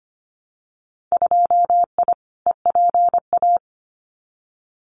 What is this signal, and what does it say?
Morse code, heard as a keyed tone, about 25 wpm, 700 Hz. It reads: 2I EPA
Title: Winter Field Day Exchanges 25wpm